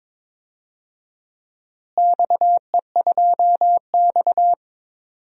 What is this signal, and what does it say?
Morse code, heard as a keyed tone, about 22 wpm, 700 Hz. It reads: XE2X
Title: DXCC-100 Call Signs - 4 Characters - Repeated 2x - 22wpm